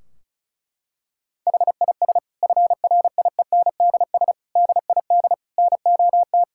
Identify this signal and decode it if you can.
Morse code, heard as a keyed tone, about 36 wpm, 700 Hz. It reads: HIS FRIENDS DID NOT
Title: Sentences from Top 1600 Words 35wpm